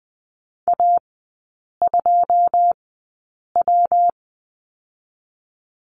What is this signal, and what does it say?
Morse code, heard as a keyed tone, about 20 wpm, spaced as for 8 wpm, 700 Hz. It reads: A2W
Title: 127 Lesson 21e - Number 6 - Three Characters 20wpm